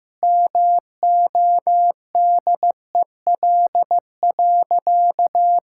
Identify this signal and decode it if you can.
Morse code, heard as a keyed tone, about 16 wpm, 700 Hz. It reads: MODEL.
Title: Sentences from Top 1000 Words - Review Part 5 - 15wpm